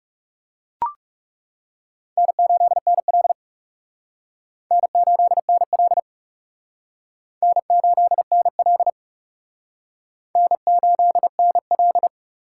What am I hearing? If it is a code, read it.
Morse code, heard as a keyed tone, about 38 wpm, 700 Hz. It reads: N8NL N8NL N8NL N8NL